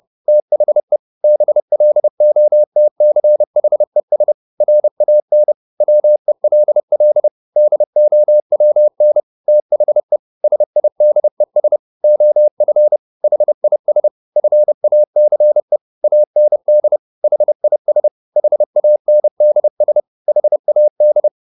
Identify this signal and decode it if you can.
Morse code, heard as a keyed tone, about 30 wpm, 600 Hz. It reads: THE BLOTCHES RAN WELL DOWN THE SIDES OF HIS FACE AND HIS HANDS HAD
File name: the old man and the sea / the old man and the sea 1 30wpm